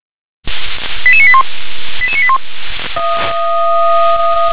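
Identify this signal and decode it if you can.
Five-tone selective call (ZVEI), heard as quick runs of five tones, tones of about 70 ms. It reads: 88981 88981
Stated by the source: ZVEI five-tone selective call 88085 received by SDR.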